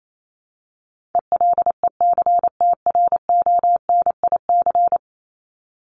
Morse code, heard as a keyed tone, about 28 wpm, 700 Hz. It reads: ELECTRONIC
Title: Top 1001-1250 Words 28wpm